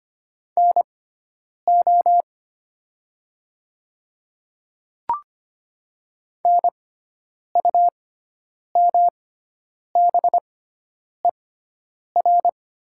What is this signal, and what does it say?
Morse code, heard as a keyed tone, about 25 wpm, spaced as for 9 wpm, 700 Hz. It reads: NO NUMBER
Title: 6x Character Spacing - Top 200 Words - 25wpm